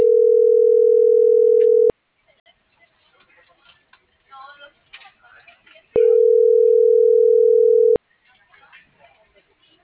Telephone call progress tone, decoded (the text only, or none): ringback tone